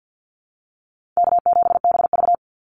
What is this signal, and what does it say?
Morse code, heard as a keyed tone, about 50 wpm, 700 Hz. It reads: K7BV
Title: Easy Call Signs Repeated 3x - 50wpm